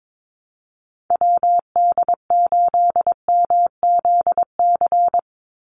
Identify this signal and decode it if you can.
Morse code, heard as a keyed tone, about 22 wpm, 700 Hz. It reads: WD8MZC
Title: USA Call Signs 22wpm